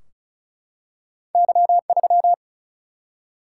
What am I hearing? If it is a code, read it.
Morse code, heard as a keyed tone, about 36 wpm, 700 Hz. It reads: Y3
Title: Single-Letter Single-Number 35wpm